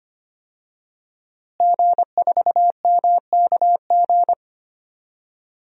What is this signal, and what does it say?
Morse code, heard as a keyed tone, about 25 wpm, 700 Hz. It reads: G4MKG
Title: DXCC-100 Call Signs - 5 Characters - Repeated 3x - 25wpm